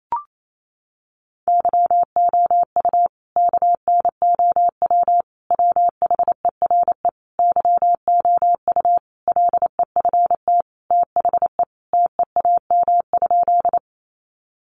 Morse code, heard as a keyed tone, about 28 wpm, 700 Hz. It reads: YOU KNOW WHERE YOU LEFT THE TEAM?
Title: Sentences from Top 1500 Words 28wpm